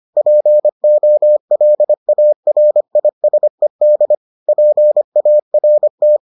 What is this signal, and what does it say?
Morse code, heard as a keyed tone, about 25 wpm, 600 Hz. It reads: POLARISED PART